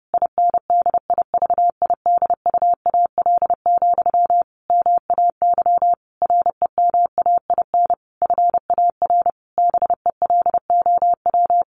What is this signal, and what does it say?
Morse code, heard as a keyed tone, about 30 wpm, 700 Hz. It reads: INDIVIDUAL, MAY REMAIN FAR BELOW
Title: Sentences from Top 1400 Words 30wpm